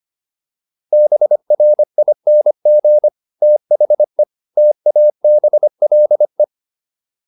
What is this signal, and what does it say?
Morse code, heard as a keyed tone, about 25 wpm, 600 Hz. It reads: BRING THE TABLE